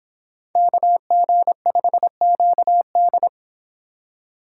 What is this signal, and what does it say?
Morse code, heard as a keyed tone, about 23 wpm, 700 Hz. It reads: KG5QD